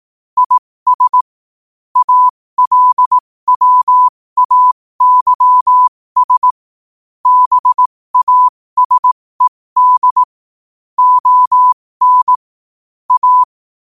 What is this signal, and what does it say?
Morse code, heard as a keyed tone, about 18 wpm, spaced as for 15 wpm, 1000 Hz. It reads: IS ALWAYS BASED ON A
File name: qotdfiles / 15wpm